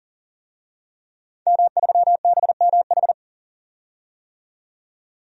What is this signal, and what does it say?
Morse code, heard as a keyed tone, about 39 wpm, 700 Hz. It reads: M3BMH
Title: DXCC-100 Call Signs - 5 Characters - 40wpm